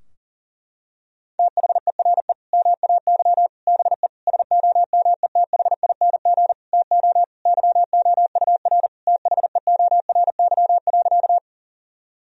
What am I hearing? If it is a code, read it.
Morse code, heard as a keyed tone, about 40 wpm, 700 Hz. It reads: THERE MAY BE SOMETHING TO YOUR THEORY.
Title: Sentences from Top 1000 Words - Review Part 3 - 40wpm